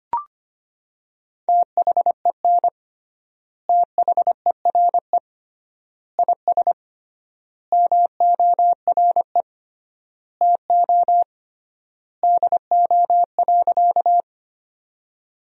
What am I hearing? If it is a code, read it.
Morse code, heard as a keyed tone, about 25 wpm, 700 Hz. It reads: THEN THERE IS MORE TO DO.
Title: 3x Word-Spacing - Sentences from Top 100 Words - 25wpm